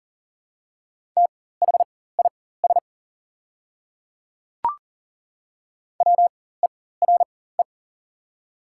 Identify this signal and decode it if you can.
Morse code, heard as a keyed tone, about 40 wpm, spaced as for 19 wpm, 700 Hz. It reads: THIS WERE
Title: 4x Character Spacing - Top 100 Words - 40wpm